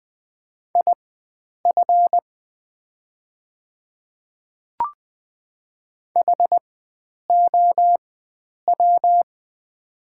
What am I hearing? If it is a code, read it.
Morse code, heard as a keyed tone, about 20 wpm, spaced as for 9 wpm, 700 Hz. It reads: IF HOW
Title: 4x Character Spacing - Top 100 Words - 20wpm